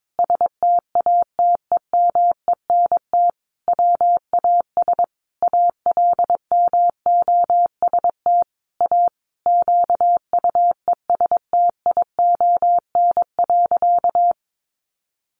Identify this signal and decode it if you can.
Morse code, heard as a keyed tone, about 22 wpm, 700 Hz. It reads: STATEMENT WAS ALMOST A QUESTION.
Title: Sentences from Top 1200 Words 22wpm